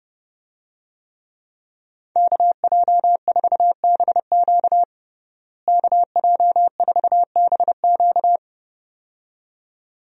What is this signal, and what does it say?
Morse code, heard as a keyed tone, about 30 wpm, 700 Hz. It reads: KJ4BQ KJ4BQ
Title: USA Call Signs - 5 Characters - Repeated 2x - 30wpm